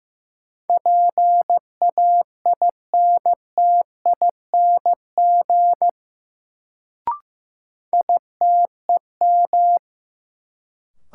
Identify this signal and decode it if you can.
Morse code, heard as a keyed tone, about 15 wpm, 700 Hz. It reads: PAINTING ITEM
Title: Sentences from Top 1000 Words - Review Part 6 - New Word Drill - 15wpm